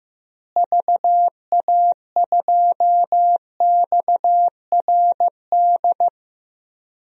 Morse code, heard as a keyed tone, about 15 wpm, 700 Hz. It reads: VA2XRD
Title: DXCC-100 Call Signs - 6 Characters - Repeated 3x - 15wpm